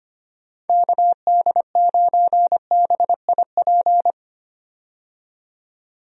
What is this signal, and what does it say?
Morse code, heard as a keyed tone, about 25 wpm, 700 Hz. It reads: KD9BIP
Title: USA Call Signs - 6 Characters - 25wpm